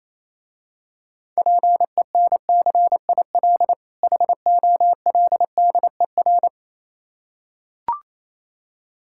Morse code, heard as a keyed tone, about 27 wpm, 700 Hz. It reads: PENCIL HOLDER
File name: Easy Words - Speed-Racing with Hints - 22wpm